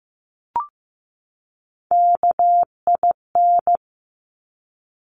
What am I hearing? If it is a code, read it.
Morse code, heard as a keyed tone, about 15 wpm, 700 Hz. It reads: KIN
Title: Top Three Letter Combinations in Words 15wpm